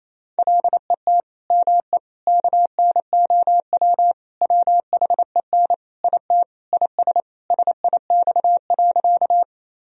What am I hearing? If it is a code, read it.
Morse code, heard as a keyed tone, about 28 wpm, 700 Hz. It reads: LET ME KNOW WHEN IT IS SIX.
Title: Sentences from Top 1000 Words - Review Part 2 - 28wpm